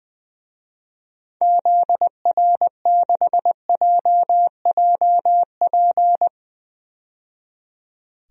Morse code, heard as a keyed tone, about 20 wpm, 700 Hz. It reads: ZR6JJP
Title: DXCC-100 Call Signs - 6 Characters - 20wpm